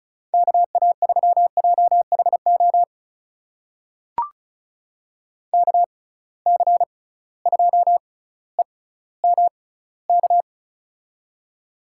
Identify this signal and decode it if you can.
Morse code, heard as a keyed tone, about 36 wpm, spaced as for 18 wpm, 700 Hz. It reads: KA3JHO KC2EMK
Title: USA Call Signs - 6 Characters - 6x Character Spacing - 35wpm